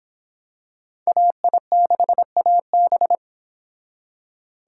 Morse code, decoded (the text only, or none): AI6AB